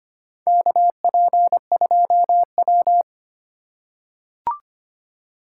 Morse code, heard as a keyed tone, about 25 wpm, 700 Hz. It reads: KP2W
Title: USA Call Signs - 4 Characters - 6x Character Spacing - 25wpm